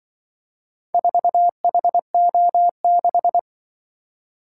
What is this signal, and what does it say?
Morse code, heard as a keyed tone, about 24 wpm, 700 Hz. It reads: 4HO6
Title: Speed-Racing - Four Letter-Number Combinations - 20wpm